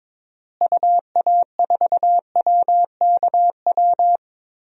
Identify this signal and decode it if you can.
Morse code, heard as a keyed tone, about 22 wpm, 700 Hz. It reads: UA4WKW
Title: DXCC-100 Call Signs - 6 Characters - Repeated 3x - 22wpm